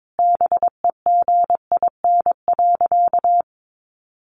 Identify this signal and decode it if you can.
Morse code, heard as a keyed tone, about 22 wpm, 700 Hz. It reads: BEGIN.
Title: Sentences from Top 1200 Words 22wpm